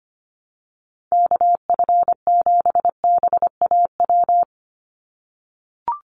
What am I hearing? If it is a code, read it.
Morse code, heard as a keyed tone, about 25 wpm, 700 Hz. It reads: KF7BAW
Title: USA Call Signs 25wpm